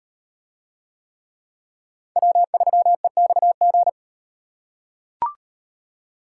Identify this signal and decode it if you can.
Morse code, heard as a keyed tone, about 38 wpm, 700 Hz. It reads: W3EXG